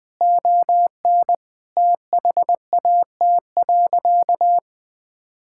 Morse code, heard as a keyed tone, about 20 wpm, 700 Hz. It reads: ON THAT.
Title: Sentences from Top 1000 Words - Review Part 5 - 20wpm